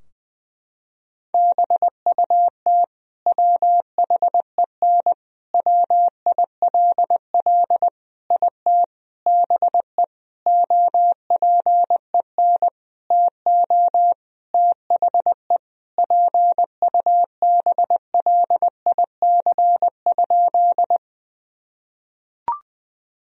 Morse code, heard as a keyed tone, about 20 wpm, 700 Hz. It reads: BUT WHEN WILL IT BE OPEN TO THE PUBLIC?
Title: Sentences from Top 1000 Words - Review Part 3 - 20wpm